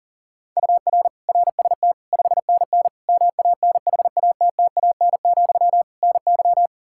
Morse code, heard as a keyed tone, about 40 wpm, 700 Hz. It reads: UR RST 5NN MANHATTAN, NY
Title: Practice QSO - Exchange 1 - Beginning - 40wpm